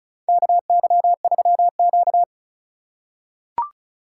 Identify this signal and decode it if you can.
Morse code, heard as a keyed tone, about 36 wpm, 700 Hz. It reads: KY3Q
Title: USA Call Signs - 2x Character Spacing - 35wpm